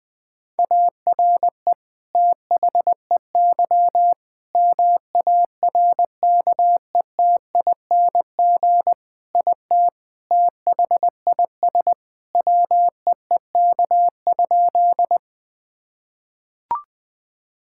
Morse code, heard as a keyed tone, about 20 wpm, 700 Hz. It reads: ARE THEY MARKETING IT THIS WEEK?
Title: Sentences from Top 1000 Words - Review Part 5 - 20wpm